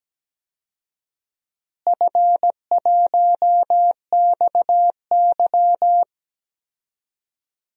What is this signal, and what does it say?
Morse code, heard as a keyed tone, about 17 wpm, 700 Hz. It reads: F1XY
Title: DXCC-100 Call Signs - 4 Characters - Repeated 2x - 17wpm